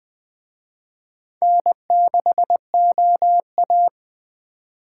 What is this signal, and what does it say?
Morse code, heard as a keyed tone, about 24 wpm, 700 Hz. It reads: N6OA
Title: Speed Racing - USA 1x2 Call Signs 20wpm